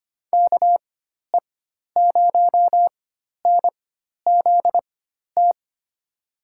Morse code, heard as a keyed tone, about 25 wpm, spaced as for 12 wpm, 700 Hz. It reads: KE0NZT
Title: USA Call Signs - 6 Characters - 4x Character Spacing - 25wpm